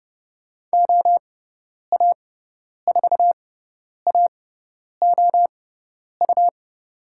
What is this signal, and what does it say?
Morse code, heard as a keyed tone, about 30 wpm, spaced as for 10 wpm, 700 Hz. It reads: OA4AOU
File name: Lesson 18g - Letter m - Call Signs 30wpm